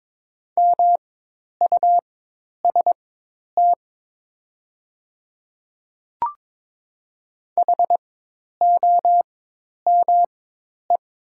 Morse code, heard as a keyed tone, about 22 wpm, spaced as for 10 wpm, 700 Hz. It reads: MUST HOME
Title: 4x Character Spacing - Top 200 Words - 22wpm